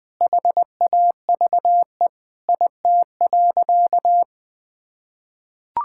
Morse code, heard as a keyed tone, about 20 wpm, 700 Hz. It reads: HAVE IT.
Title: Sentences with 2-Word Phrases - Spoken Beginning - 20wpm